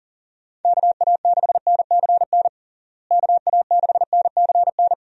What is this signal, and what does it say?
Morse code, heard as a keyed tone, about 40 wpm, 700 Hz. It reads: KA6NCN KA6NCN
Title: USA Call Signs - 6 Characters - Repeated 2x - 40wpm